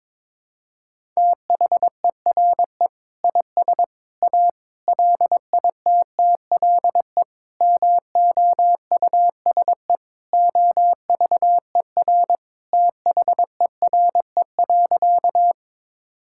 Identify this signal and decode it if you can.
Morse code, heard as a keyed tone, about 22 wpm, 700 Hz. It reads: THERE IS A LITTLE MOUSE OVER THERE.
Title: Sentences from Top 1000 Words - Review Part 5 - 22wpm